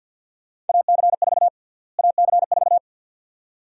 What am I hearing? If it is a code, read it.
Morse code, heard as a keyed tone, about 49 wpm, 700 Hz. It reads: AC4 AC4
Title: USA Call Sign Prefixes - Repeated 2x - 50wpm